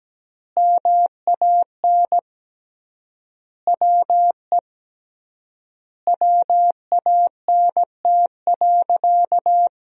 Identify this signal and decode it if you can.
Morse code, heard as a keyed tone, about 17 wpm, 700 Hz. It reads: MAN WE WANT.
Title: 3x Word-Spacing - Sentences from Top 200 Words - 17wpm